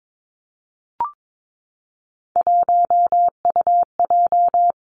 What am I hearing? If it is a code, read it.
Morse code, heard as a keyed tone, about 22 wpm, 700 Hz. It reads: 1UJ
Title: Three Challenging Letter-Number Combinations 22wpm